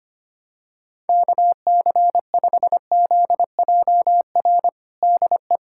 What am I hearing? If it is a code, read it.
Morse code, heard as a keyed tone, about 25 wpm, 700 Hz. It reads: KC5ZJR DE
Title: USA Callsign DE Callsign - 6 Characters - 25wpm